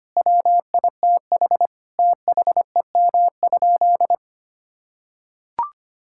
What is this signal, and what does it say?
Morse code, heard as a keyed tone, about 25 wpm, 700 Hz. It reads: WITH THEM?
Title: Sentences from Top 1000 Words - Review Part 5 - 25wpm